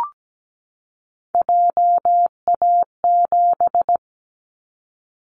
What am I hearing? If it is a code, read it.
Morse code, heard as a keyed tone, about 17 wpm, 700 Hz. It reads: JA7
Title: Callsign Prefixes for Top 50 DXCC - 17wpm